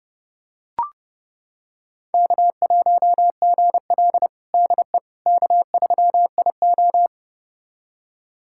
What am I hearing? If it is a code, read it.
Morse code, heard as a keyed tone, about 30 wpm, 700 Hz. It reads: K1GL DE K3IO
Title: USA Callsign DE Callsign - 4 Characters - 30wpm